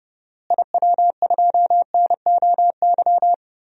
Morse code, heard as a keyed tone, about 31 wpm, 700 Hz. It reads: IW2NOY
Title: DXCC-100 Call Signs - 6 Characters - Repeated 2x - 30wpm